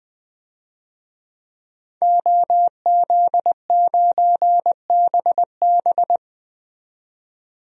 Morse code, heard as a keyed tone, about 20 wpm, 700 Hz. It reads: OZ9BB
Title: DXCC-100 Call Signs - 5 Characters - 20wpm